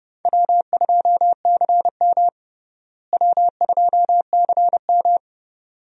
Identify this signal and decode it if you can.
Morse code, heard as a keyed tone, about 30 wpm, 700 Hz. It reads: W2CM W2CM
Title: USA Call Signs - 4 Characters - Repeated 2x - 30wpm